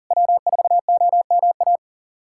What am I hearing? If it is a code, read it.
Morse code, heard as a keyed tone, about 43 wpm, 700 Hz. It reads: W4OMA
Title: SOTA QSO Practice 40wpm